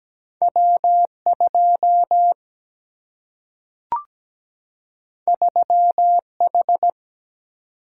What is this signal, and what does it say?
Morse code, heard as a keyed tone, about 17 wpm, 700 Hz. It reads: W2 3H
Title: Single-Letter Single-Number 17wpm